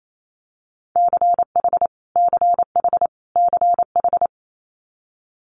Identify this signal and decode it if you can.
Morse code, heard as a keyed tone, about 28 wpm, 700 Hz. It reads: CH CH CH
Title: Multi-letter Phonemes Repeated 3x 28wpm